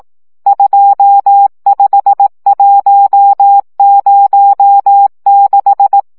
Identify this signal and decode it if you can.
Morse code, heard as a keyed tone, about 18 wpm, 800 Hz. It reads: 25106
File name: morze vizsga